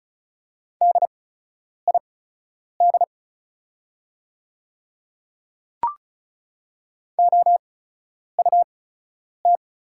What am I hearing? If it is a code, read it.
Morse code, heard as a keyed tone, about 36 wpm, spaced as for 10 wpm, 700 Hz. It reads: DID OUT
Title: 8x Character Spacing - Top 100 Words - 35wpm